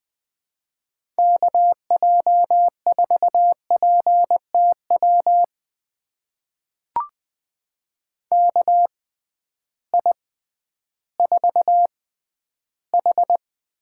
Morse code, heard as a keyed tone, about 21 wpm, spaced as for 12 wpm, 700 Hz. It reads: KJ4PTW KI4H
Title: USA Call Signs - 6x Character Spacing - 20wpm